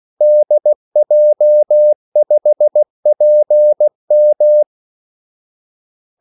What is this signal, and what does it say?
Morse code, heard as a keyed tone, about 16 wpm, 600 Hz. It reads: DJ5PM
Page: Kaldesignaler 16 wpm